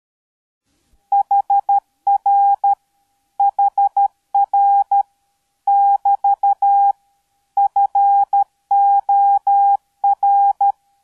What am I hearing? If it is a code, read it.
Morse code, heard as a keyed tone, about 13 wpm, 800 Hz. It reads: HR HR <BT> FOR